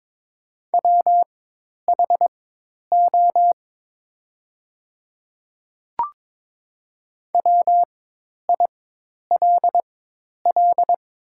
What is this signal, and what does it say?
Morse code, heard as a keyed tone, about 22 wpm, spaced as for 10 wpm, 700 Hz. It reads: WHO WILL
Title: 4x Character Spacing - Top 100 Words - 22wpm